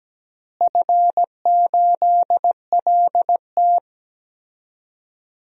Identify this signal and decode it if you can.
Morse code, heard as a keyed tone, about 17 wpm, 700 Hz. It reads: F8LT
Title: DXCC-100 Call Signs - 4 Characters - Repeated 2x - 17wpm